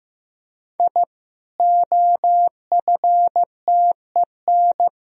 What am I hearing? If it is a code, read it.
Morse code, heard as a keyed tone, about 15 wpm, 700 Hz. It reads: I OFTEN
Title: Sentences from Top 1000 Words - Review Part 1 - 15wpm